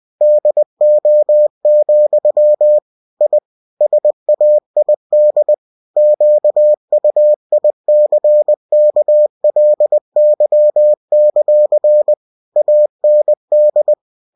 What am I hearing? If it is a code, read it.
Morse code, heard as a keyed tone, about 20 wpm, 600 Hz. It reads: DO, I SAID QUICKLY; AND